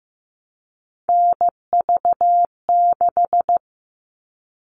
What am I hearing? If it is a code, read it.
Morse code, heard as a keyed tone, about 15 wpm, 700 Hz. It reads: NV6
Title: All Cut Numbers - 3-Digit 15wpm